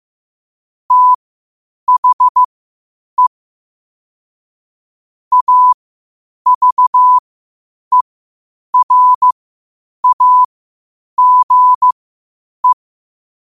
Morse code, heard as a keyed tone, about 15 wpm, spaced as for 8 wpm, 1000 Hz. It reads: THE AVERAGE
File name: qotdfiles / 8wpm